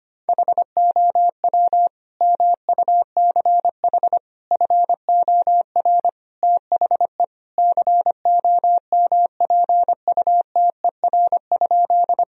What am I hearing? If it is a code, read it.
Morse code, heard as a keyed tone, about 25 wpm, 700 Hz. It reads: HOW MUCH FOR THE COMPUTER?
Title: Sentences from Top 1000 Words - Review Part 3 - 25wpm